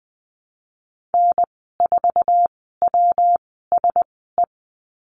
Morse code, heard as a keyed tone, about 20 wpm, spaced as for 15 wpm, 700 Hz. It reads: N4WSE
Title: 161 Lesson 26g - Letter g - Call Signs 20wpm